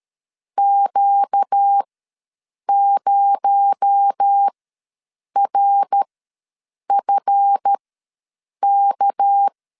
Morse code, heard as a keyed tone, about 13 wpm, spaced as for 7 wpm, 800 Hz. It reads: Q0RFK
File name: LEZIONI / 7 GF9Z / 7 GF ws 12wpm 3140